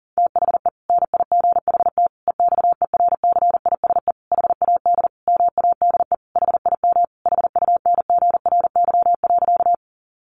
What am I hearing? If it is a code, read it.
Morse code, heard as a keyed tone, about 40 wpm, 700 Hz. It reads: THE NIGHT EXERCISE HAD MADE HIM HUNGRY.
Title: Sentences from Top 1400 Words 40wpm